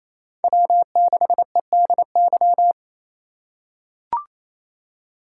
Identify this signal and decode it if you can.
Morse code, heard as a keyed tone, about 27 wpm, 700 Hz. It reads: W6EDY
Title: K1USN SST QSO Practice - 28wpm